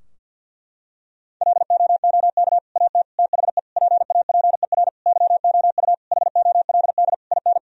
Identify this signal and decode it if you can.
Morse code, heard as a keyed tone, about 47 wpm, 700 Hz. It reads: LOOK AT THE PAPER YOU HOLD IN
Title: Sentences from Top 1000 Words - Review Part 2 - 50wpm